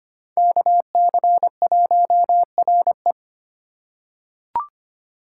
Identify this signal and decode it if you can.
Morse code, heard as a keyed tone, about 25 wpm, 700 Hz. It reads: KC1RE
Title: USA Call Signs - 5 Characters - 3x Character Spacing - 25wpm